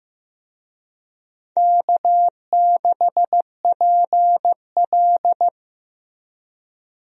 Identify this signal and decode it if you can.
Morse code, heard as a keyed tone, about 15 wpm, 700 Hz. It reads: K6PL
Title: USA Call Signs - 4 Characters - Repeated 2x - 15wpm